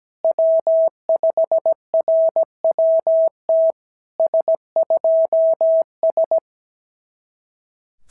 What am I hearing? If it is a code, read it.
Morse code, heard as a keyed tone, about 17 wpm, 660 Hz. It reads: W5RWT S2S
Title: SOTA QSO Practice 17wpm